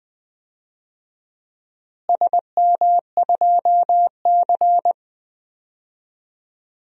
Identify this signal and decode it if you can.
Morse code, heard as a keyed tone, about 20 wpm, 700 Hz. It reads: SM2C
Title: DXCC-100 Call Signs - 4 Characters - Repeated 3x - 20wpm